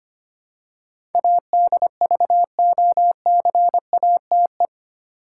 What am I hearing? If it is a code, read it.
Morse code, heard as a keyed tone, about 25 wpm, 700 Hz. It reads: ADVOCATE
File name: 8 Letter Words - First 3 Letters Given - 25wpm